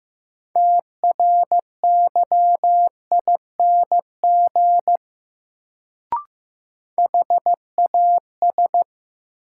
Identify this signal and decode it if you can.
Morse code, heard as a keyed tone, about 15 wpm, 700 Hz. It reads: TRYING HAS
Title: Sets of 2 Words - New Word Drill - 15wpm